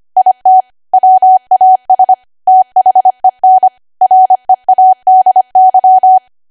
Morse code, heard as a keyed tone, about 25 wpm, 750 Hz. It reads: IT WAS THEN READY